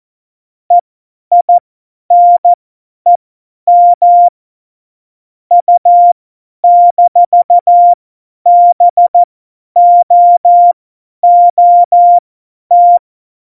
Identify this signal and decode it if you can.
Morse code, heard as a keyed tone, about 14 wpm, spaced as for 10 wpm, 700 Hz. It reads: EINEM U-BOOT
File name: Morsekurs / Witze / Witz 03.Juli 2018, 50 F70